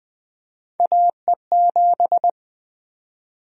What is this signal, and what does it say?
Morse code, heard as a keyed tone, about 20 wpm, 700 Hz. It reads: AE7
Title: USA Call Sign Prefixes - Repeated 2x - 20wpm